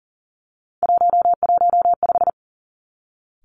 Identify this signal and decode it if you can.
Morse code, heard as a keyed tone, about 42 wpm, 690 Hz. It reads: 115
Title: Three Challenging Letter-Number Combinations 40wpm